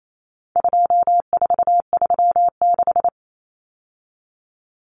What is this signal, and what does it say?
Morse code, heard as a keyed tone, about 28 wpm, 700 Hz. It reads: 2436
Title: Numbers - 4-Digit 28wpm